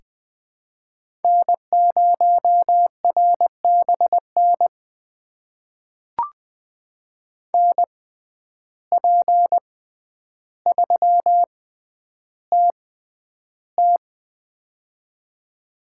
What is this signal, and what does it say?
Morse code, heard as a keyed tone, about 20 wpm, spaced as for 10 wpm, 700 Hz. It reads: N0RBN NP3TT
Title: USA Call Signs - 5 Characters - 6x Character Spacing - 20wpm